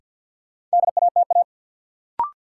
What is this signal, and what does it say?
Morse code, heard as a keyed tone, about 53 wpm, 700 Hz. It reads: DATA
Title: Sentences from Top 1000 Words - Review Part 3 - New Word Drill - 50wpm